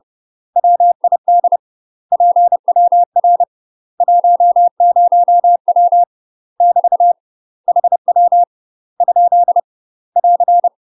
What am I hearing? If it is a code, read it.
Morse code, heard as a keyed tone, about 30 wpm, 700 Hz. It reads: WID PWR 10W <BT> HW ? <AR>
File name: UK MASTER QSOs TRAINING / 30WPM / UK MASTER QSOs 0039